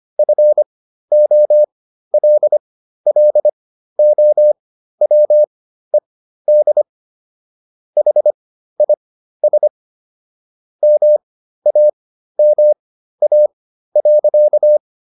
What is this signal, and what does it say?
Morse code, heard as a keyed tone, about 25 wpm, spaced as for 13 wpm, 600 Hz. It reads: FOLLOWED HIS MAMA.